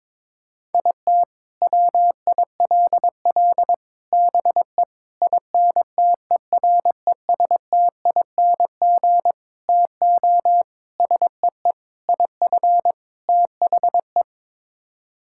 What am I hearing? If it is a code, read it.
Morse code, heard as a keyed tone, about 22 wpm, 700 Hz. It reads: IT WILL BE INTERESTING TO SEE IF THE
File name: practice / 22wpm / Sets of 8 Words - 22wpm